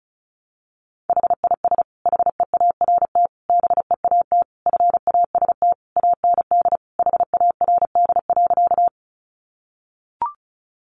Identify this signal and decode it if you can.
Morse code, heard as a keyed tone, about 34 wpm, 700 Hz. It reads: HIS HEART BEAT FAST AND HARD.
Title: Sentences from Top 1300 Words 35wpm